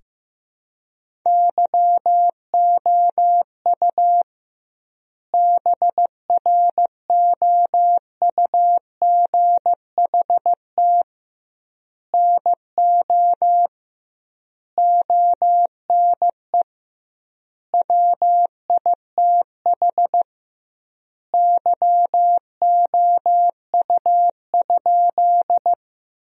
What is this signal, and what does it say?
Morse code, heard as a keyed tone, about 15 wpm, 700 Hz. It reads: YOU BROUGHT NO ONE WITH YOU?
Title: 2x Word-Spacing - Sentences from Top 500 Words - 15wpm